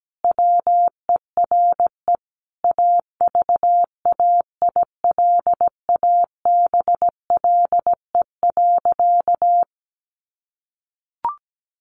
Morse code, heard as a keyed tone, about 17 wpm, 700 Hz. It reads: WERE AVAILABLE.
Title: Sentences from Top 1100 Words 17wpm